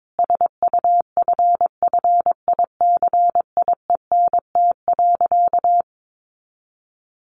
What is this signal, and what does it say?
Morse code, heard as a keyed tone, about 22 wpm, 700 Hz. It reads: SUFFICIENT.
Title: Sentences from Top 1400 Words 22wpm